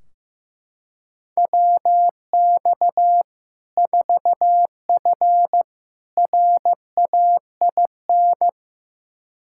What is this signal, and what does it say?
Morse code, heard as a keyed tone, about 15 wpm, 700 Hz. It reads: WX 4F RAIN
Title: Practice QSO - Exchange 2 - Info - 15wpm